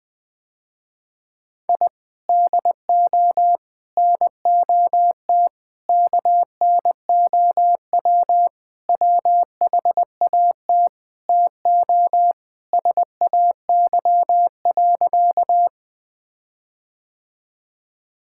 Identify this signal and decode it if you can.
Morse code, heard as a keyed tone, about 21 wpm, 700 Hz. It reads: I DO NOT KNOW WHAT TO SAY.
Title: Sentences with 5-Word Phrases - Spoken Beginning - 20wpm